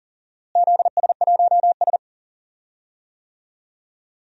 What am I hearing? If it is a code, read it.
Morse code, heard as a keyed tone, about 40 wpm, 700 Hz. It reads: ZS1S
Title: DXCC-100 Call Signs - 4 Characters - Repeated 2x - 40wpm